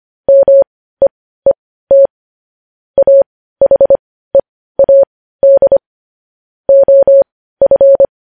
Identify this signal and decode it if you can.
Morse code, heard as a keyed tone, about 25 wpm, spaced as for 15 wpm, 550 Hz. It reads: MEET AHEAD OF
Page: news-headlines-in-morse-code-at-15-wpm